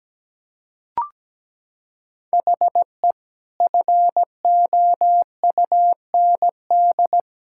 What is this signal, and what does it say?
Morse code, heard as a keyed tone, about 17 wpm, 700 Hz. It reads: HE FOUND
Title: Sentences from Top 1000 Words - Review Part 6 - 17wpm